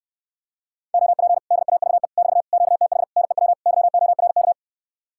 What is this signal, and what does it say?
Morse code, heard as a keyed tone, about 67 wpm, 700 Hz. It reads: OP DAVE <BT> QTH NEW YORK